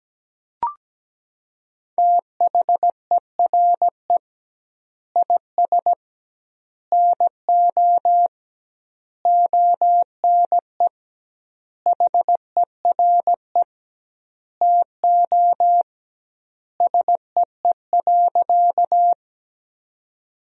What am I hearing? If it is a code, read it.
Morse code, heard as a keyed tone, about 17 wpm, 700 Hz. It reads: THERE IS NO ONE HERE TO SEE.
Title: 2x Word-Spacing - Sentences from Top 200 Words - 17wpm